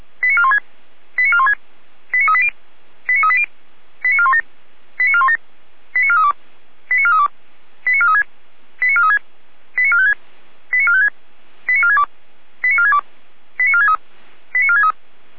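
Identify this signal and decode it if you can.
Five-tone selective call (ZVEI), heard as quick runs of five tones, tones of about 70 ms. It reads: 89416 89416 89380 89380 89417 89417 89432 89432 89436 89436 89456 89456 89462 89462 89463 89463
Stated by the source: Five-tone ZVEI call 89402 received on an SDR.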